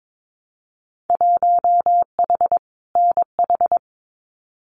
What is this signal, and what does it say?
Morse code, heard as a keyed tone, about 22 wpm, 700 Hz. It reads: 1H NH
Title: Winter Field Day Exchanges 22wpm